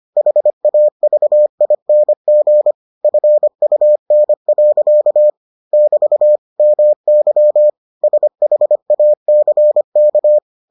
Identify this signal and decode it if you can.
Morse code, heard as a keyed tone, about 25 wpm, 600 Hz. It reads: HAVING FUN. <BT> MY SHACK